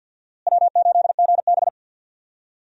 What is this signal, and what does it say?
Morse code, heard as a keyed tone, about 52 wpm, 700 Hz. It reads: W8GB
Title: USA Call Signs - 4 Characters - Repeated 2x - 50wpm